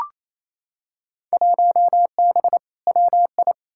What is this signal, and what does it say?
Morse code, heard as a keyed tone, about 28 wpm, 700 Hz. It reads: 1B WI
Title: Field Day Exchanges Encore - 28wpm